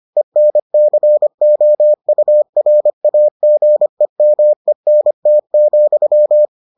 Morse code, heard as a keyed tone, about 25 wpm, 600 Hz. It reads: ENCOURAGEMENT,